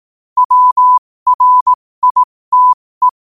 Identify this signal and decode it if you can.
Morse code, heard as a keyed tone, about 18 wpm, spaced as for 15 wpm, 1000 Hz. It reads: WRITE
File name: qotdfiles / 15wpm